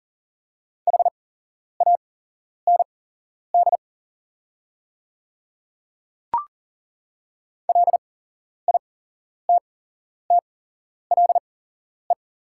Morse code, heard as a keyed tone, about 40 wpm, spaced as for 11 wpm, 700 Hz. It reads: HAND LITTLE
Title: 8x Character Spacing - Top 300 Words - 40wpm